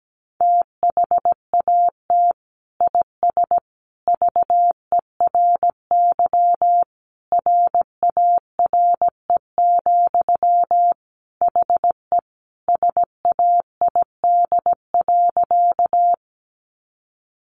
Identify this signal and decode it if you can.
Morse code, heard as a keyed tone, about 17 wpm, 700 Hz. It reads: THAT IS VERY RARE, HE SAID.
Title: Sentences from Top 1100 Words 17wpm